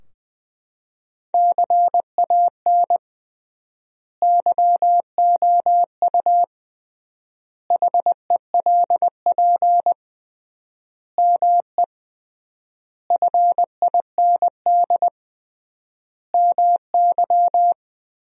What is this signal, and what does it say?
Morse code, heard as a keyed tone, about 20 wpm, 700 Hz. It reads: CAN YOU HELP ME FIND MY
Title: 3x Word-Spacing - Sentences from Top 200 Words - 20wpm